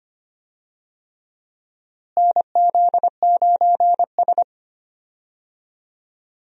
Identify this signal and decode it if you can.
Morse code, heard as a keyed tone, about 25 wpm, 700 Hz. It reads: NZ9S
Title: USA Call Signs - 4 Characters - Repeated 3x - 25wpm